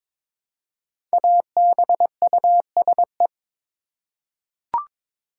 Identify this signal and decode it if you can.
Morse code, heard as a keyed tone, about 22 wpm, 700 Hz. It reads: ABUSE
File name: 5 Letter Words - First 3 Letters Given - 22wpm